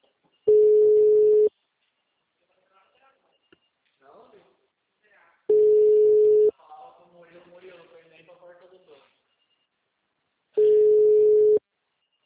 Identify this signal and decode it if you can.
Ringback tone (Europe, 425 Hz), with speech.
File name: out- -2016- - - .72806